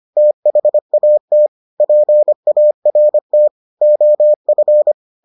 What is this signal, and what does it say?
Morse code, heard as a keyed tone, about 25 wpm, 600 Hz. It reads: THAT PART OF